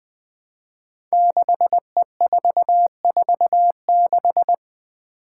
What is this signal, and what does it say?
Morse code, heard as a keyed tone, about 21 wpm, 700 Hz. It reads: 6E446
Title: Five Challenging Letter-Number Combinations 20wpm